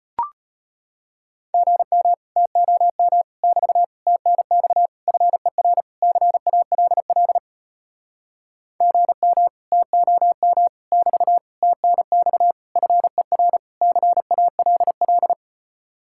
Morse code, heard as a keyed tone, about 33 wpm, 700 Hz. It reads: GM TOM <BT> TNX FER CALL GM TOM <BT> TNX FER CALL